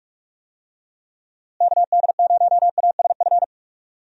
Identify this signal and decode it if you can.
Morse code, heard as a keyed tone, about 46 wpm, 700 Hz. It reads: KD0ASF
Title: USA Call Signs - 6 Characters - Repeated 3x - 45wpm